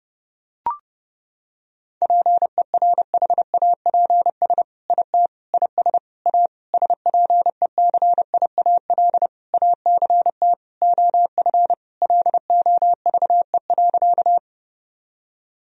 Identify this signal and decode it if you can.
Morse code, heard as a keyed tone, about 30 wpm, 700 Hz. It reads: PERHAPS IT IS A SPECIAL ACT OF LOVE.
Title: Sentences from Top 1000 Words - Review Part 3 - 30wpm